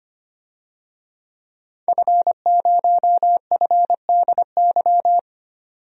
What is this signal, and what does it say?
Morse code, heard as a keyed tone, about 25 wpm, 700 Hz. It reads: F0FDY
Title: DXCC-100 Call Signs - 5 Characters - 25wpm